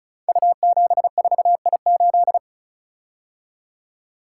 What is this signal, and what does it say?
Morse code, heard as a keyed tone, about 36 wpm, 690 Hz. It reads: U74I8
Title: Code Groups - 5 Letter:Numbers - 35wpm